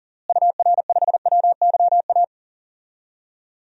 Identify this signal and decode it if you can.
Morse code, heard as a keyed tone, about 40 wpm, 700 Hz. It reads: UR5WYA
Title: DXCC-100 Call Signs - 6 Characters - Repeated 2x - 40wpm